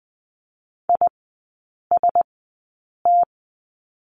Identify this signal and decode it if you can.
Morse code, heard as a keyed tone, about 20 wpm, spaced as for 8 wpm, 700 Hz. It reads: IST
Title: 053 Lesson 10e - Letter r - Three Characters 20wpm